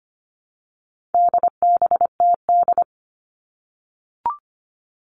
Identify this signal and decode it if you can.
Morse code, heard as a keyed tone, about 25 wpm, 700 Hz. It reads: DBTD
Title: All Cut Numbers - 4-Digit 25wpm